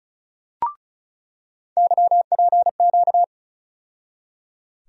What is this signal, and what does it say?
Morse code, heard as a keyed tone, about 36 wpm, 700 Hz. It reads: YPQ
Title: USA Call Sign Postfixes - 3 Letters 35wpm